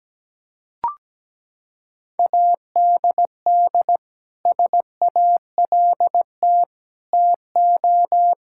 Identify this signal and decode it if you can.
Morse code, heard as a keyed tone, about 17 wpm, 700 Hz. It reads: ADD SALT TO
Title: Sentences from Top 1000 Words - Review Part 5 - 17wpm